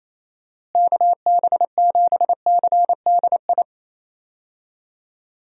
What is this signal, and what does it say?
Morse code, heard as a keyed tone, about 28 wpm, 700 Hz. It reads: KB7CDI
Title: USA Call Signs - 6 Characters - Repeated 3x - 28wpm